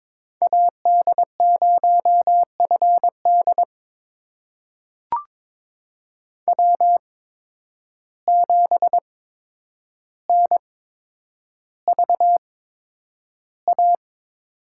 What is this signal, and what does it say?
Morse code, heard as a keyed tone, about 22 wpm, spaced as for 9 wpm, 700 Hz. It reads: AD0FD W7NVA
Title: USA Call Signs - 8x Character Spacing - 22wpm